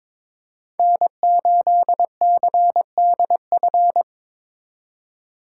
Morse code, heard as a keyed tone, about 26 wpm, 700 Hz. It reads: N8CDF
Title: Speed Racing - USA 1x3 Call Signs 22wpm